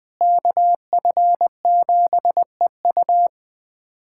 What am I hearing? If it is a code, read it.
Morse code, heard as a keyed tone, about 21 wpm, 700 Hz. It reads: KF7EU
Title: USA Callsign DE Callsign - 5 Characters - 20wpm